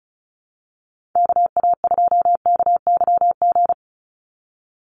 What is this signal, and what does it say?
Morse code, heard as a keyed tone, about 34 wpm, 700 Hz. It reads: KA2KYG
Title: USA Call Signs 35wpm